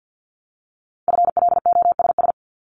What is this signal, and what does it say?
Morse code, heard as a keyed tone, about 50 wpm, 700 Hz. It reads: FLOSS
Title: Multi-letter Phonemes Repeated 3x 50wpm